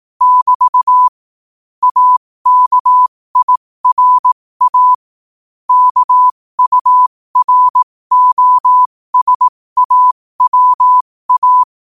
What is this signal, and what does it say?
Morse code, heard as a keyed tone, about 18 wpm, spaced as for 15 wpm, 1000 Hz. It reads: <BT> AKIRA KUROSAWA
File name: qotdfiles / 15wpm